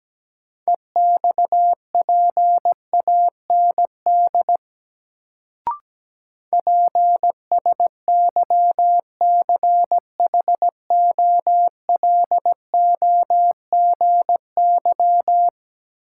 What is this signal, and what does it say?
Morse code, heard as a keyed tone, about 17 wpm, 700 Hz. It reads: EXPAND PSYCHOLOGY
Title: Sentences from Top 1400 Words - New Word Drill - 17wpm